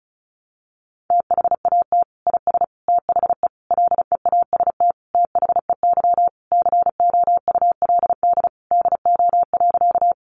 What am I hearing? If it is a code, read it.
Morse code, heard as a keyed tone, about 34 wpm, 700 Hz. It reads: THAT IS THE LEAST THEY COULD DO.
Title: Sentences from Top 700 Words 35wpm